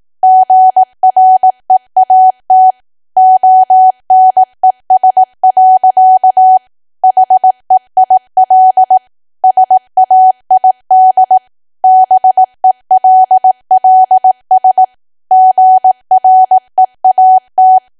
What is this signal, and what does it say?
Morse code, heard as a keyed tone, about 18 wpm, 750 Hz. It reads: GREAT ONES. HEIL SAID BELLS GREAT
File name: Archive / 18 WPM / 18WPM